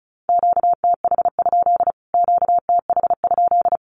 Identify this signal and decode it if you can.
Morse code, heard as a keyed tone, about 34 wpm, 700 Hz. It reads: QTH? QTH?
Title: Q-Codes Repeated 3x 35wpm